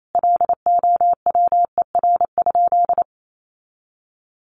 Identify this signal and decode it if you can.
Morse code, heard as a keyed tone, about 28 wpm, 700 Hz. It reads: LOWER?
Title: Sentences from Top 700 Words 28wpm